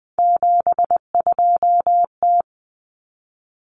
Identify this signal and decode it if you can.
Morse code, heard as a keyed tone, about 20 wpm, 700 Hz. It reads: 72T
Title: Common Cut Numbers - 3-Digit 20wpm